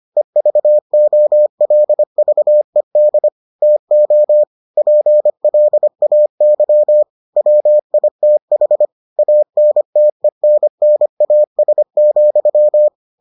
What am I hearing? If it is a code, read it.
Morse code, heard as a keyed tone, about 25 wpm, 600 Hz. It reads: EVOLVED TO PLAY WITH ANTENNAS,